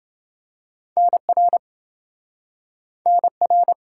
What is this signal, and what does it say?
Morse code, heard as a keyed tone, about 28 wpm, 700 Hz. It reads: NR NR